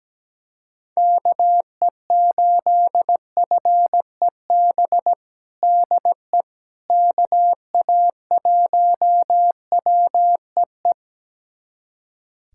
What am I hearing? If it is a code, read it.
Morse code, heard as a keyed tone, about 17 wpm, 700 Hz. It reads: KE8FEB DE KA1WEE
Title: USA Callsign DE Callsign - 6 Characters - 17wpm